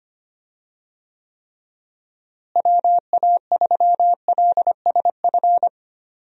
Morse code, heard as a keyed tone, about 25 wpm, 700 Hz. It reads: WA3LSF
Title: USA Call Signs - 6 Characters - Repeated 2x - 25wpm